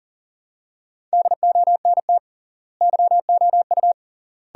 Morse code, heard as a keyed tone, about 40 wpm, 700 Hz. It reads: DONT YOU
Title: 3x Word-Spacing - Sentences from Top 300 Words - 40wpm